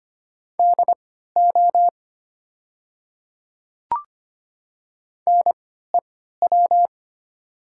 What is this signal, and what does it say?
Morse code, heard as a keyed tone, about 25 wpm, spaced as for 14 wpm, 700 Hz. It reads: DO NEW
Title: 3x Character Spacing - Top 300 Words - 25wpm